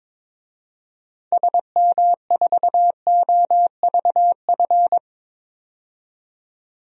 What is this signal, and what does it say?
Morse code, heard as a keyed tone, about 22 wpm, 700 Hz. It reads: SM4OVF
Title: DXCC-100 Call Signs - 6 Characters - 22wpm